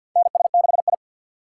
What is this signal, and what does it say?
Morse code, heard as a keyed tone, about 47 wpm, 700 Hz. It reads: NS6I
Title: USA Call Signs - 6x Character Spacing - 50wpm